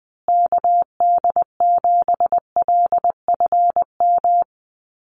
Morse code, heard as a keyed tone, about 20 wpm, 700 Hz. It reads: KD7LFM
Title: USA Call Signs Repeated 2x 20wpm